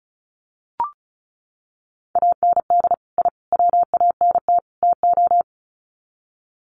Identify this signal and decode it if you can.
Morse code, heard as a keyed tone, about 34 wpm, 700 Hz. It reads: AND I WANT TO
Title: Sets of 4 Words 35wpm